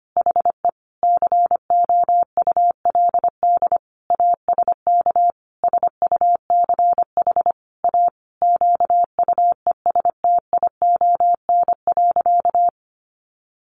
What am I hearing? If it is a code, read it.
Morse code, heard as a keyed tone, about 25 wpm, 700 Hz. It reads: HE COULD ASK SUCH A QUESTION.
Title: Sentences from Top 600 Words 25wpm